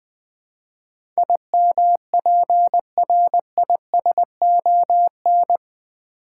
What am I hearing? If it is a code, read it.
Morse code, heard as a keyed tone, about 20 wpm, 700 Hz. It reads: IMPRISON
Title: Build A Word - 3 Letters At A Time - 8-Letter Words - 20wpm